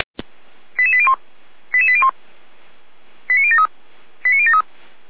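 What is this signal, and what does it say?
Five-tone selective call (ZVEI), heard as quick runs of five tones, tones of about 70 ms. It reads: 88801 88801 89073 89073
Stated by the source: Received by SDR; ZVEI five-tone call, Feuerwehr Orlamünde fire service alerting.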